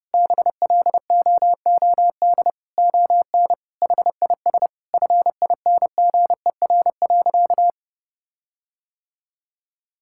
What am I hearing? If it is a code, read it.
Morse code, heard as a keyed tone, about 32 wpm, 700 Hz. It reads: BLOOD ON HIS FINGER.
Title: Sentences from Top 1600 Words 30wpm